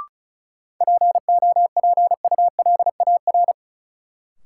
Morse code, heard as a keyed tone, about 36 wpm, 700 Hz. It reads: POPULAR
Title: Sentences from Top 1000 Words - Review Part 3 - New Word Drill - 35wpm